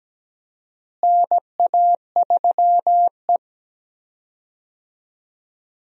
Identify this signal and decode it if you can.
Morse code, heard as a keyed tone, about 17 wpm, 700 Hz. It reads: NA3E
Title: USA Call Signs - 4 Characters - Repeated 3x - 17wpm